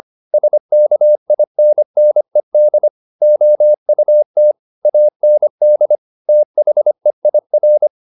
Morse code, heard as a keyed tone, about 25 wpm, 600 Hz. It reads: SKINNED OUT AND THEIR